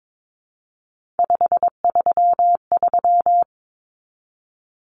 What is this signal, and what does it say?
Morse code, heard as a keyed tone, about 22 wpm, 700 Hz. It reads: 533
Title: Numbers - 3-Digit 22wpm